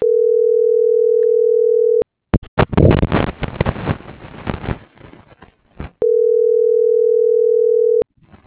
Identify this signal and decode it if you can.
Ringback tone (North America), 440 + 480 Hz.